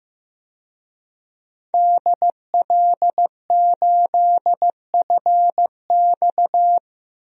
Morse code, heard as a keyed tone, about 15 wpm, 700 Hz. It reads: DL8FX
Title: DXCC-100 Call Signs - 5 Characters - Repeated 3x - 15wpm